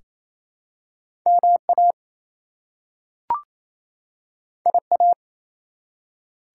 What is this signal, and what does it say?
Morse code, heard as a keyed tone, about 28 wpm, 700 Hz. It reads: MA IA
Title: US States and Canadian Provinces 28wpm